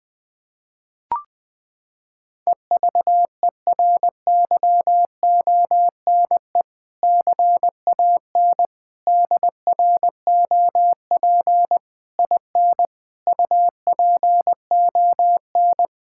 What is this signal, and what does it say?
Morse code, heard as a keyed tone, about 20 wpm, 700 Hz. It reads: EVERYONE CAN DROP IN UPON
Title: Sentences from Top 1600 Words 20wpm